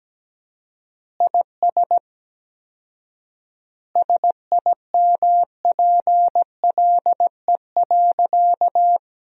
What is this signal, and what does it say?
Morse code, heard as a keyed tone, about 17 wpm, 700 Hz. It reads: IS SIMPLE.
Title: 4x Word-Spacing - Sentences from Top 400 Words - 17wpm